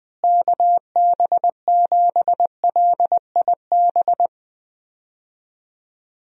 Morse code, heard as a keyed tone, about 20 wpm, 700 Hz. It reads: KB7LIB
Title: USA Call Signs - 6 Characters - Repeated 2x - 20wpm